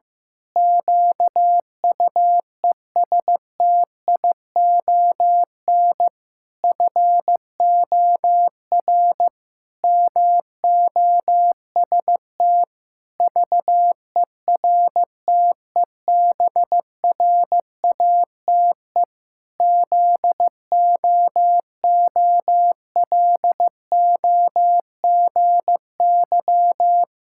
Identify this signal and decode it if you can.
Morse code, heard as a keyed tone, about 15 wpm, 700 Hz. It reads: QUESTION FOR MOST VERTEBRATE ZOOLOGY
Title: Pangram Challenge v2 - 15wpm